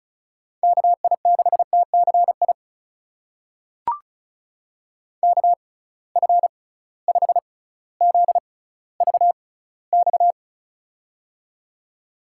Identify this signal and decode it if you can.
Morse code, heard as a keyed tone, about 36 wpm, spaced as for 18 wpm, 700 Hz. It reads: KI6TCI KF5ZVX
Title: USA Call Signs - 6 Characters - 6x Character Spacing - 35wpm